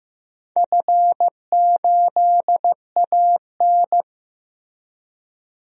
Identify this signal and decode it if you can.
Morse code, heard as a keyed tone, about 15 wpm, 700 Hz. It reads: F8AN
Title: DXCC-100 Call Signs - 4 Characters - Repeated 2x - 15wpm